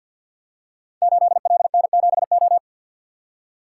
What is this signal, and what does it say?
Morse code, heard as a keyed tone, about 48 wpm, 690 Hz. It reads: 8LN7O
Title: Code Groups - 5 Letter:Numbers - 50wpm